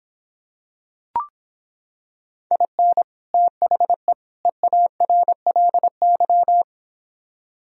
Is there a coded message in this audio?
Yes: Morse code from 2.5 to 6.6 s, heard as a keyed tone, about 26 wpm, 700 Hz. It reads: IN THE EARLY